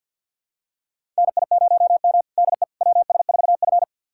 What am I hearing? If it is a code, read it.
Morse code, heard as a keyed tone, about 52 wpm, 700 Hz. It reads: NI0M DE WS4F
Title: USA Callsign DE Callsign - 4 Characters - 50wpm